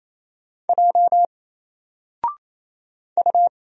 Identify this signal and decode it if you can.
Morse code, heard as a keyed tone, about 27 wpm, 700 Hz. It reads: J U
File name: Single-Letter-Number-28wpm